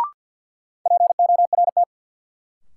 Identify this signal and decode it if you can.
Morse code, heard as a keyed tone, about 53 wpm, 700 Hz. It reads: PORT
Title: Sentences from Top 1000 Words - Review Part 1 - New Word Drill - 50wpm